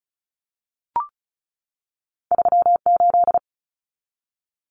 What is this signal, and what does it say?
Morse code, heard as a keyed tone, about 36 wpm, 700 Hz. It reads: 38
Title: Numbers - 2-Digit 35wpm